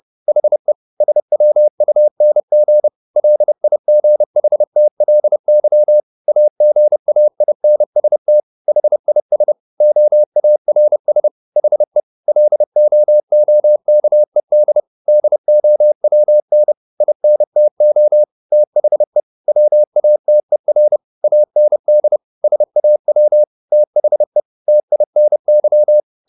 Morse code, heard as a keyed tone, about 30 wpm, 600 Hz. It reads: HE SWUNG LIGHTLY AGAINST HIS OARS HE LOOKED DOWN INTO THE WATER AND SAW THE TINY
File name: the old man and the sea / the old man and the sea 1 30wpm